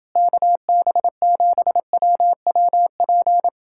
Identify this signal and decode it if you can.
Morse code, heard as a keyed tone, about 29 wpm, 700 Hz. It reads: KB7WWP